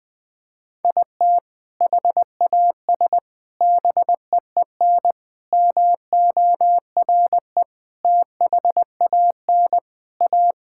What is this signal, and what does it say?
Morse code, heard as a keyed tone, about 21 wpm, 700 Hz. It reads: IT HAS BEEN MORE THAN A
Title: Sentences with 4-Word Phrases - 20wpm